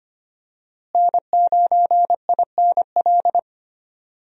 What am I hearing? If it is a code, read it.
Morse code, heard as a keyed tone, about 25 wpm, 700 Hz. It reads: N9INL
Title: USA Call Signs - 5 Characters - Repeated 3x - 25wpm